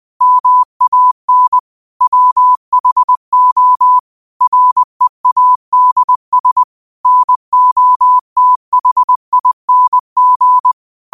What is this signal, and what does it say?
Morse code, heard as a keyed tone, about 20 wpm, 1000 Hz. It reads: MAN WHO READS NOTHING